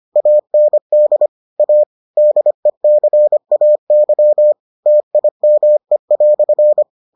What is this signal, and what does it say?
Morse code, heard as a keyed tone, about 25 wpm, 600 Hz. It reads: AND A DECAY TIME"